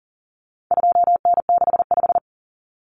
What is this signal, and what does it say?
Morse code, heard as a keyed tone, about 40 wpm, 700 Hz. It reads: 2N65
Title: Common Cut Numbers - 4-Digit 40wpm